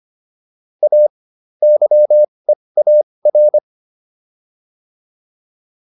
Morse code, heard as a keyed tone, about 25 wpm, spaced as for 20 wpm, 600 Hz. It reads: A YEAR